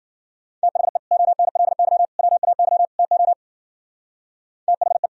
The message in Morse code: THE ONLY WAY TO THE

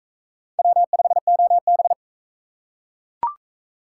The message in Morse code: W5OB

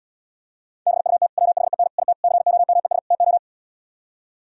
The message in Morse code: BUT WHEN IT COMES TO

